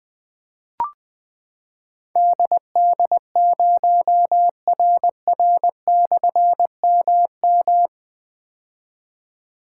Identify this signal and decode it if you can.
Morse code, heard as a keyed tone, about 20 wpm, 700 Hz. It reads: DD0RR/MM
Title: Difficult Call Signs 20wpm